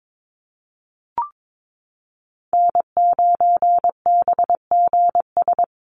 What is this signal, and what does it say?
Morse code, heard as a keyed tone, about 22 wpm, 700 Hz. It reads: N9BGS
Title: USA Call Signs Repeated 3x 22wpm